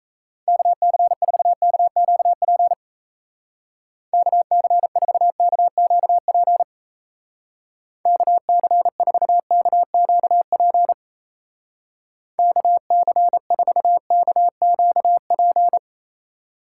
Morse code, heard as a keyed tone, about 36 wpm, 700 Hz. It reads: KC4KQP KC4KQP KC4KQP KC4KQP